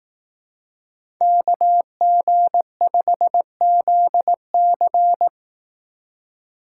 Morse code, heard as a keyed tone, about 18 wpm, 700 Hz. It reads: KG5ZC
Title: Speed Racing - USA 2x2 Call Signs 15wpm